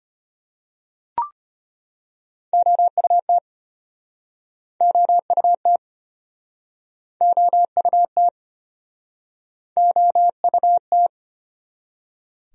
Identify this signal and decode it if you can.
Morse code, heard as a keyed tone, about 33 wpm, 700 Hz. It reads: OUT OUT OUT OUT